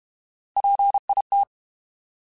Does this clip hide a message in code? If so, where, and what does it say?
Yes: Morse code from 0.6 to 1.4 s, heard as a keyed tone, about 32 wpm, 800 Hz. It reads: PIT